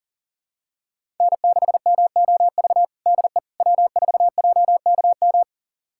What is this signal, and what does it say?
Morse code, heard as a keyed tone, about 37 wpm, 700 Hz. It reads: N6MOV DE W4JKM
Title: USA Callsign DE Callsign - 5 Characters - 40wpm